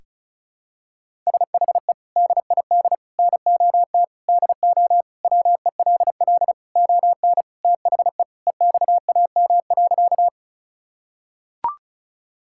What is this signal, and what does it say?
Morse code, heard as a keyed tone, about 34 wpm, 700 Hz. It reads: SHE DID NOT DO WELL ON THE EXAM.
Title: Sentences from Top 1000 Words - Review Part 5 - 35wpm